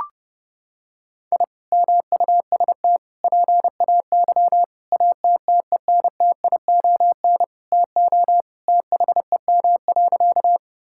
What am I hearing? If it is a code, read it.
Morse code, heard as a keyed tone, about 30 wpm, 700 Hz. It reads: I MUST PAY ATTENTION TO THEM.
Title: Sentences from Top 1000 Words - Review Part 6 - 30wpm